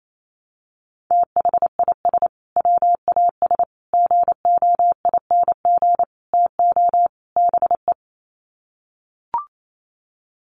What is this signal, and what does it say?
Morse code, heard as a keyed tone, about 28 wpm, 700 Hz. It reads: THIS WAS GOING TO BE
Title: Sets of 5 Words Encore2 28wpm